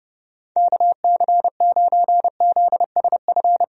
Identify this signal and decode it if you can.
Morse code, heard as a keyed tone, about 30 wpm, 700 Hz. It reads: KC9ZSF
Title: USA Call Signs - 6 Characters - 30wpm